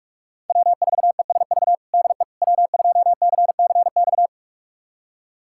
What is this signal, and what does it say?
Morse code, heard as a keyed tone, about 46 wpm, 700 Hz. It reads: W4ESV DE W2CCX
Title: USA Callsign DE Callsign - 5 Characters - 45wpm